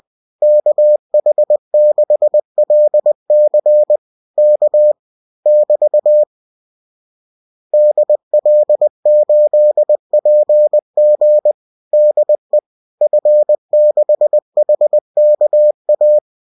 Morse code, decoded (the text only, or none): KH6LC K <BT> DL8PG DE F6HKA